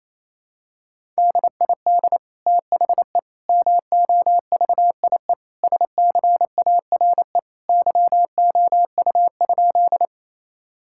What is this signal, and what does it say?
Morse code, heard as a keyed tone, about 28 wpm, 700 Hz. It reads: DID THE MOVIE SCARE YOU?
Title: Sentences from Top 1000 Words - Review Part 5 - 28wpm